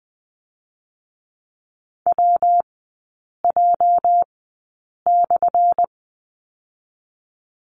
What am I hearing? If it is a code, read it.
Morse code, heard as a keyed tone, about 20 wpm, spaced as for 8 wpm, 700 Hz. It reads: WJ/
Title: 205 Lesson 33e - Letter j - Three Characters 20wpm